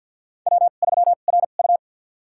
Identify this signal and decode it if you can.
Morse code, heard as a keyed tone, about 52 wpm, spaced as for 36 wpm, 700 Hz. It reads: W3RU
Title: USA Call Signs - 4 Characters - 2x Character Spacing - 50wpm